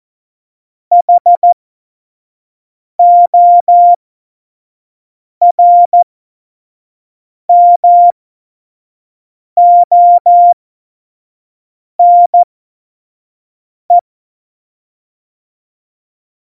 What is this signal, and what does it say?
Morse code, heard as a keyed tone, about 14 wpm, spaced as for 5 wpm, 700 Hz. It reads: HORMONE
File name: Morsekurs / Witze / Witz 19 Mai 2020, 25 F70